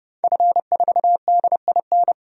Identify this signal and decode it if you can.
Morse code, heard as a keyed tone, about 30 wpm, 700 Hz. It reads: F4DIN
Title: DXCC-100 Call Signs - 5 Characters - 30wpm